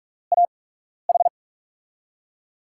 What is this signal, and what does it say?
Morse code, heard as a keyed tone, about 47 wpm, spaced as for 12 wpm, 700 Hz. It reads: AH
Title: USA Call Signs - 6 Characters - 8x Character Spacing - 45wpm